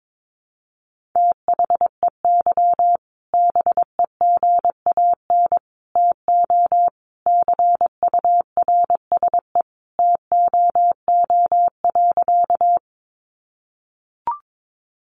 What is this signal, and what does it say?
Morse code, heard as a keyed tone, about 22 wpm, 700 Hz. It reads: THEY BEGAN TO CURSE TOO.
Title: A Sound Of Thunder 22wpm